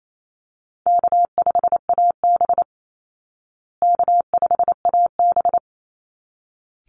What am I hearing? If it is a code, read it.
Morse code, heard as a keyed tone, about 28 wpm, 700 Hz. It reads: K5AB K5AB
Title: Call Signs Repeated 3x - 28wpm